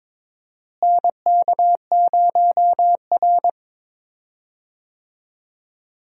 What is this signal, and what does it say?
Morse code, heard as a keyed tone, about 22 wpm, 700 Hz. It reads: NK0R
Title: USA Call Signs - 4 Characters - Repeated 2x - 22wpm